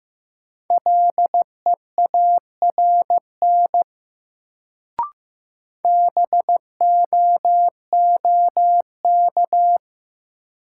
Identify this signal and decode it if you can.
Morse code, heard as a keyed tone, about 15 wpm, 700 Hz. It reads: LEARN BOOK
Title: Sentences from Top 300 Words - New Word Drill - 15wpm